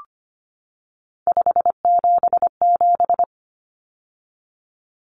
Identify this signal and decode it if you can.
Morse code, heard as a keyed tone, about 25 wpm, 700 Hz. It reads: 577
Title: Common Cut Numbers - 3-Digit 25wpm